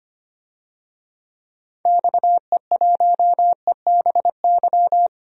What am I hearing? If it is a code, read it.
Morse code, heard as a keyed tone, about 25 wpm, 700 Hz. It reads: XE1EBY
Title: DXCC-100 Call Signs - 6 Characters - 25wpm